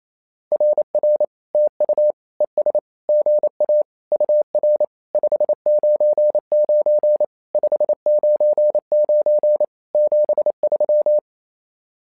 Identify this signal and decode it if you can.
Morse code, heard as a keyed tone, about 27 wpm, 600 Hz. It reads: RR TU ES GA UR 599 599 73
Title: SOTA QSO Practice 28wpm